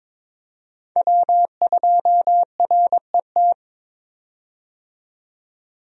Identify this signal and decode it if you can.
Morse code, heard as a keyed tone, about 22 wpm, 700 Hz. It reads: W2RET
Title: USA Call Signs - 5 Characters - 22wpm